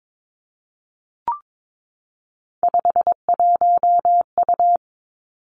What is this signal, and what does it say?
Morse code, heard as a keyed tone, about 22 wpm, 700 Hz. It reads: 51U
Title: Three Challenging Letter-Number Combinations 22wpm